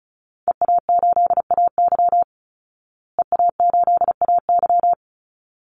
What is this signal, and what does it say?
Morse code, heard as a keyed tone, about 34 wpm, 700 Hz. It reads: EA8AY EA8AY
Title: Call Signs Repeated 3x - 35wpm